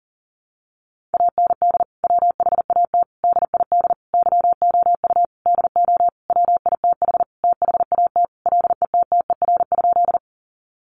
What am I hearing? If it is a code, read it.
Morse code, heard as a keyed tone, about 40 wpm, 700 Hz. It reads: AND WHAT DID YOU DO WITH THAT LETTER?
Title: Sentences from Top 300 Words 40wpm